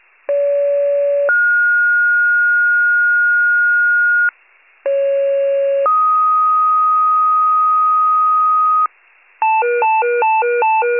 Two-tone page, heard as a one-second tone, then a three-second tone, about 570 then 1430 Hz, about 550 then 1250 Hz, with noise. It is heard from 0.3 to 8.9 s.
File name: pager / 28 FIRE / wcalert